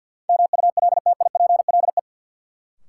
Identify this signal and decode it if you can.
Morse code, heard as a keyed tone, about 51 wpm, 700 Hz. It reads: MULTIPLE
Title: Sets of 5 Words - New Word Drill - 50wpm